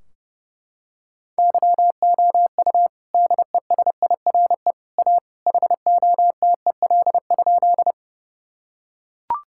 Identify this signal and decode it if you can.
Morse code, heard as a keyed tone, about 30 wpm, 700 Hz. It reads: YOU DESIRE A HOTEL?
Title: Sentences from Top 1600 Words 30wpm